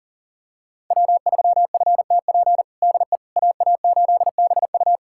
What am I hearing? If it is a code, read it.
Morse code, heard as a keyed tone, about 40 wpm, 700 Hz. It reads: W3FTP DE AA8BU
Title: USA Callsign DE Callsign - 5 Characters - 40wpm